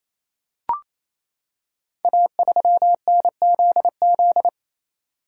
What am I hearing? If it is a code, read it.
Morse code, heard as a keyed tone, about 29 wpm, 700 Hz. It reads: A3NZZ
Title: Code Groups - 5 Letter:Numbers - 28wpm